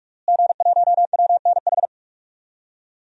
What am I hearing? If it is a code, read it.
Morse code, heard as a keyed tone, about 46 wpm, 700 Hz. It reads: G1WNH
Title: DXCC-100 Call Signs - 5 Characters - Repeated 3x - 45wpm